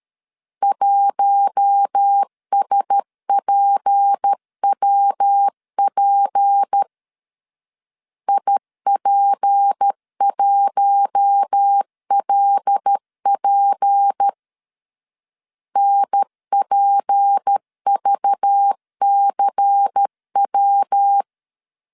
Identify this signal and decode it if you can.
Morse code, heard as a keyed tone, about 13 wpm, 800 Hz. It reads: 1SPWP IP1LP NPVCW
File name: LEZIONI / 6 PKX0 / 6 P 12wpm 2033